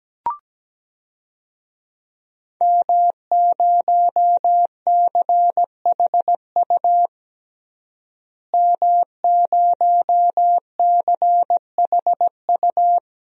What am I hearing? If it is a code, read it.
Morse code, heard as a keyed tone, about 17 wpm, 700 Hz. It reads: M0CHU M0CHU
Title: DXCC-100 Call Signs - 5 Characters - Repeated 2x - 17wpm